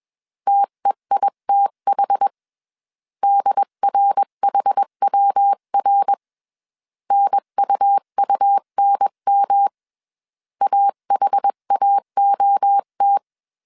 Morse code, heard as a keyed tone, about 21 wpm, 800 Hz. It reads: TEITH DRHWR NUUNM AHAOT
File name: 20wpm